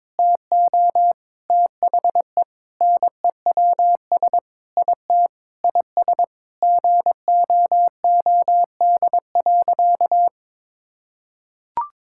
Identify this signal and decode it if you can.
Morse code, heard as a keyed tone, about 21 wpm, 700 Hz. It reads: TO THE NEWS IT IS GOOD.
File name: practice / 22wpm / Sentences with 3-Word Phrases - 22wpm